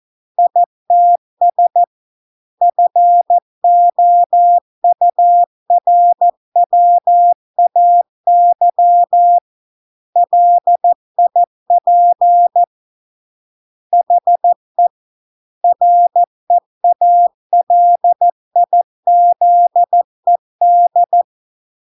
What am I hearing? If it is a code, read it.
Morse code, heard as a keyed tone, about 14 wpm, 700 Hz. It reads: ITS FOURWAY LIP HE REALIZED